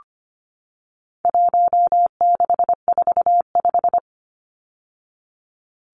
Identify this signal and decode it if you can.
Morse code, heard as a keyed tone, about 25 wpm, 700 Hz. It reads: 1645
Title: Numbers - 4-Digit 25wpm